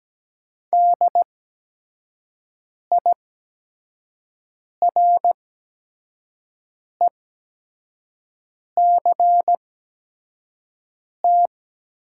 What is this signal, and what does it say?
Morse code, heard as a keyed tone, about 17 wpm, spaced as for 5 wpm, 700 Hz. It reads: DIRECT
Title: 8x Character Spacing - Top 500 Words - 17wpm